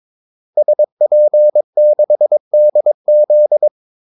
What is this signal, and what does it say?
Morse code, heard as a keyed tone, about 22 wpm, 600 Hz. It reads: SP6DZ
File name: Callsigns-22wpm